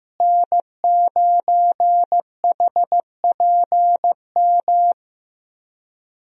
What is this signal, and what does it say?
Morse code, heard as a keyed tone, about 15 wpm, 700 Hz. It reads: N9HPM
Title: USA Call Signs - 5 Characters - Repeated 3x - 15wpm